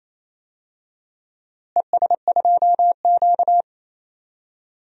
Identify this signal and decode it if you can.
Morse code, heard as a keyed tone, about 28 wpm, 700 Hz. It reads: ES2Q
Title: DXCC-100 Call Signs - 4 Characters - Repeated 2x - 28wpm